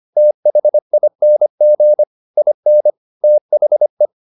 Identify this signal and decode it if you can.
Morse code, heard as a keyed tone, about 25 wpm, 600 Hz. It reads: THING IN THE
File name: Never 0025wpm